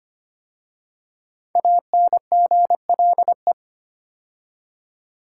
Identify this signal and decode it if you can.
Morse code, heard as a keyed tone, about 25 wpm, 700 Hz. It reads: ANGLE
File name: 5 Letter Words - First 3 Letters Given - 25wpm